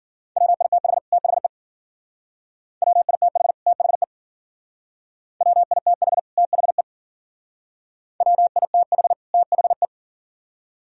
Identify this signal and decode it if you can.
Morse code, heard as a keyed tone, about 51 wpm, 700 Hz. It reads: WITH THE WITH THE WITH THE WITH THE